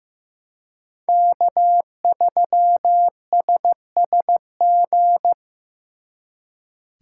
Morse code, heard as a keyed tone, about 18 wpm, 700 Hz. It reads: K3SSG
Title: Speed Racing - USA 1x3 Call Signs 15wpm